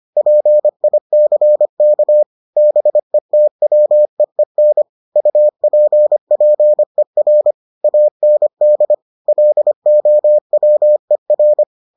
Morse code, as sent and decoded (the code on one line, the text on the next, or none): .--. .. -.-. -.- / -... - .-- . -. / ..- .--. .--. .-. / .- -. -.. / .-.. --- .-- . .-.
PICK BETWEEN UPPER AND LOWER